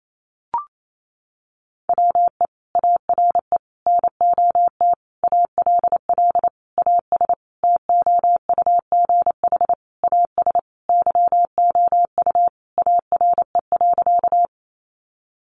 Morse code, heard as a keyed tone, about 28 wpm, 700 Hz. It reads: WE ARE NOT ALL AS TOUGH AS YOU ARE.
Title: Sentences from Top 1200 Words 28wpm